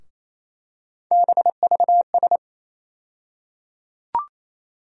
Morse code, heard as a keyed tone, about 28 wpm, 700 Hz. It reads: BVS
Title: USA Call Sign Postfixes - 3 Letters 28wpm